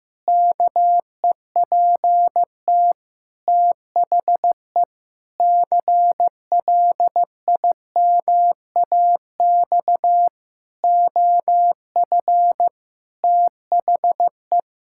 Morse code, heard as a keyed tone, about 15 wpm, 700 Hz. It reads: KEPT THE CLIMAX OF THE
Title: Pangram Challenge v2 - 15wpm